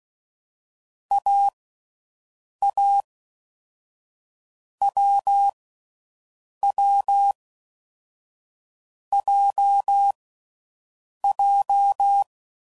Morse code, heard as a keyed tone, about 16 wpm, 800 Hz. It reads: A A W W J J
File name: Mixed / Short Runs / awjndb6gz7